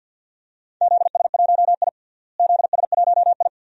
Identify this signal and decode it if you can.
Morse code, heard as a keyed tone, about 53 wpm, 700 Hz. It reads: ZS1I ZS1I
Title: DXCC-100 Call Signs - 4 Characters - Repeated 2x - 50wpm